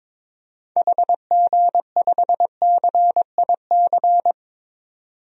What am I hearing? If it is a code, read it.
Morse code, heard as a keyed tone, about 22 wpm, 700 Hz. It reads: HG5CIC
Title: DXCC-100 Call Signs - 6 Characters - 22wpm